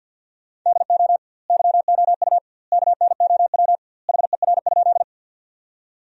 Morse code, heard as a keyed tone, about 50 wpm, 700 Hz. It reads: DO YOU KNOW HER?
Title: 2x Word-Spacing - Sentences from Top 100 Words - 50wpm